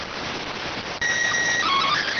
SSTV (slow-scan television), PD 290.